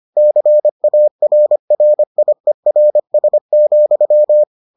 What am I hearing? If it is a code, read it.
Morse code, heard as a keyed tone, about 25 wpm, 600 Hz. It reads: CARRIERS,